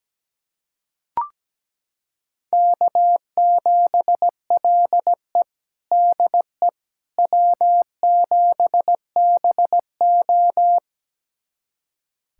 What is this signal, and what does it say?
Morse code, heard as a keyed tone, about 17 wpm, 700 Hz. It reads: K7LE DE W7BO
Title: USA Callsign DE Callsign - 4 Characters - 17wpm